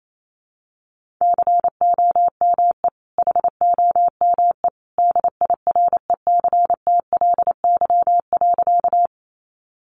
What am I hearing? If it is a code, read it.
Morse code, heard as a keyed tone, about 28 wpm, 700 Hz. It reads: COME HOME DIRECTLY.
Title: Sentences from Top 800 Words 28wpm